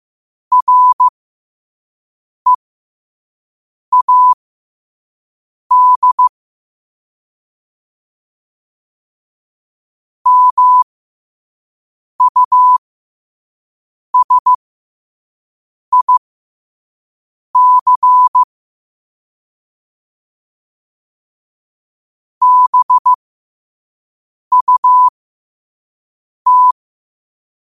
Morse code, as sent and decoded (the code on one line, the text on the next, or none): .-. .- -.. / -- ..- ... .. -.-. / -... ..- -
READ MUSIC BUT